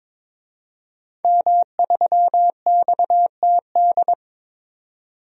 Morse code, heard as a keyed tone, about 21 wpm, 700 Hz. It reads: M3XTD
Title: DXCC-100 Call Signs - 5 Characters - 22wpm